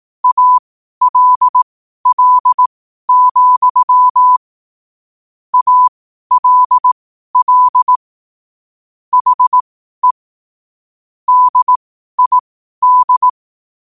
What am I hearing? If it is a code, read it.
Morse code, heard as a keyed tone, about 18 wpm, spaced as for 12 wpm, 1000 Hz. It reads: ALL, ALL HE DID
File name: qotdfiles / 12wpm